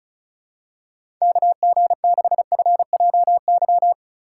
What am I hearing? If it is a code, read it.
Morse code, heard as a keyed tone, about 34 wpm, 700 Hz. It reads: KG6FJY
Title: USA Call Signs - 6 Characters - 4x Character Spacing - 35wpm